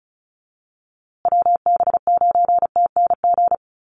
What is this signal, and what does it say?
Morse code, heard as a keyed tone, about 34 wpm, 700 Hz. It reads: WB9TNG
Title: USA Call Signs Repeated 3x 35wpm